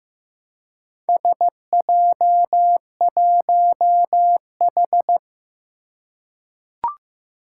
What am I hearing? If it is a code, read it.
Morse code, heard as a keyed tone, about 15 wpm, 700 Hz. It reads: SJ1H
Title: Four Challenging Letter-Number Combinations 15wpm